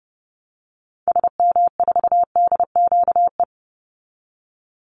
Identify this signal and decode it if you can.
Morse code, heard as a keyed tone, about 30 wpm, 700 Hz. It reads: SM4DQE
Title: Call Signs Repeated 3x - 30wpm